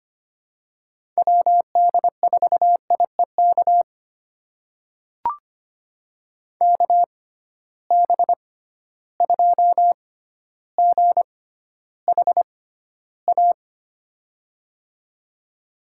Morse code, heard as a keyed tone, about 25 wpm, spaced as for 13 wpm, 700 Hz. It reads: WD4IEK KB2GHA
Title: USA Call Signs - 6 Characters - 6x Character Spacing - 25wpm